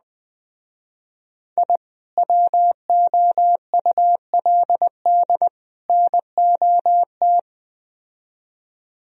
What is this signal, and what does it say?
Morse code, heard as a keyed tone, about 21 wpm, 700 Hz. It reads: I WOULD NOT